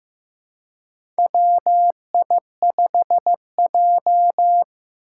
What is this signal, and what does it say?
Morse code, heard as a keyed tone, about 15 wpm, 700 Hz. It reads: WI5J
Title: Four Challenging Letter-Number Combinations 15wpm